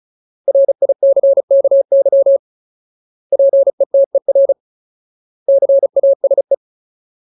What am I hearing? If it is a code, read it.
Morse code, heard as a keyed tone, about 34 wpm, 550 Hz. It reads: RICKY PETER CASE
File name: Ultimate Word List / 35WPM Ultimate